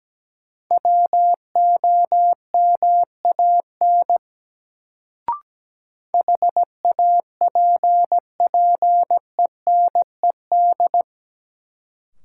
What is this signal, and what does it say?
Morse code, heard as a keyed tone, about 17 wpm, 700 Hz. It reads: WOMAN HAPPENED
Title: Sentences from Top 1000 Words - Review Part 2 - New Word Drill - 17wpm